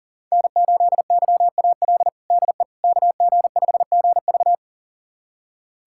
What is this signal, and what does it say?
Morse code, heard as a keyed tone, about 40 wpm, 700 Hz. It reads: N8YAL DE KG5GV
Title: USA Callsign DE Callsign - 5 Characters - 40wpm